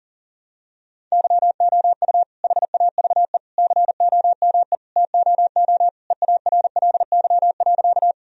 Morse code, heard as a keyed tone, about 40 wpm, 700 Hz. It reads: YOU HAVE COME TOO EARLY.
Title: Sentences from Top 1000 Words - Review Part 2 - 40wpm